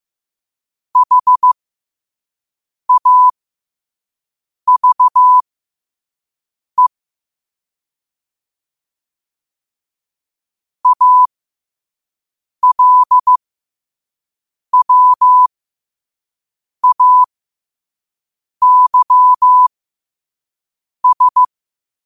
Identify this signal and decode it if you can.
Morse code, heard as a keyed tone, about 15 wpm, spaced as for 5 wpm, 1000 Hz. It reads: HAVE ALWAYS
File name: 5wpm